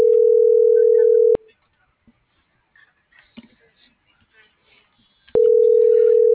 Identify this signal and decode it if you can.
Ringback tone (North America), 440 + 480 Hz.
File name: CENTRAL SOTOVA / out alejandro.rodriguez - .56542